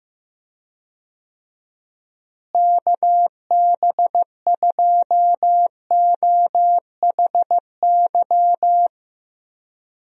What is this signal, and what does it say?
Morse code, heard as a keyed tone, about 15 wpm, 700 Hz. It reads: KB2OHY
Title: USA Call Signs - 6 Characters - Repeated 3x - 15wpm